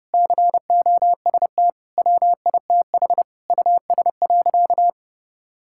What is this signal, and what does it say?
Morse code, heard as a keyed tone, about 32 wpm, 700 Hz. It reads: COST WITH US.
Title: Sentences from Top 1000 Words - Review Part 6 - 30wpm